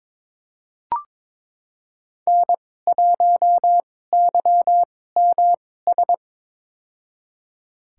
Morse code, heard as a keyed tone, about 22 wpm, spaced as for 16 wpm, 700 Hz. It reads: N1YMS
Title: USA Call Signs - 2x Character Spacing - 22wpm